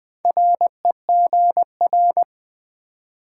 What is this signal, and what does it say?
Morse code, heard as a keyed tone, about 20 wpm, 690 Hz. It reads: REGR
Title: Build A Word - 2 Letters At A Time - 6-Letter Words - 20wpm